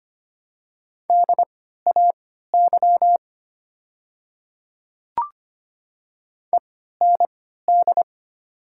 Morse code, heard as a keyed tone, about 25 wpm, spaced as for 14 wpm, 690 Hz. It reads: DAY END
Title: 3x Character Spacing - Top 300 Words - 25wpm